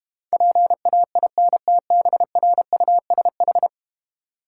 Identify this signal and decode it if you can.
Morse code, heard as a keyed tone, about 33 wpm, 700 Hz. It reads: PAINTBRUSH